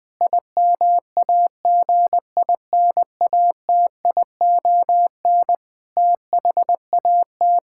Morse code, heard as a keyed tone, about 20 wpm, 700 Hz. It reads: IMAGINATION THAT
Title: Difficult Sentences with Hints - 20wpm